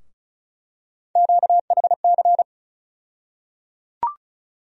Morse code, heard as a keyed tone, about 36 wpm, 700 Hz. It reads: QHC
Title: USA Call Sign Postfixes - 3 Letters 35wpm